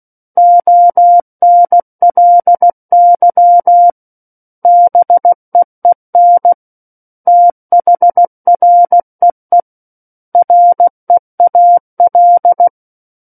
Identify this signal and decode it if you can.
Morse code, heard as a keyed tone, about 16 wpm, 700 Hz. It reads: ONLY BEEN THREE REAL
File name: Book One / 07-Chapter 7 - 16 WPM